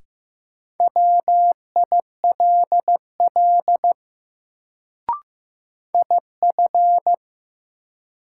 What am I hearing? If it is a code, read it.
Morse code, heard as a keyed tone, about 15 wpm, 700 Hz. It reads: WILL IF
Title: Sets of 2 Words - New Word Drill - 15wpm